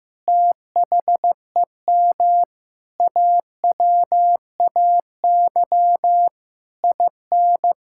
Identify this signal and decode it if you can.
Morse code, heard as a keyed tone, about 15 wpm, 700 Hz. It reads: THEM AWAY IN
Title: Sentences from Top 1000 Words - Review Part 5 - 15wpm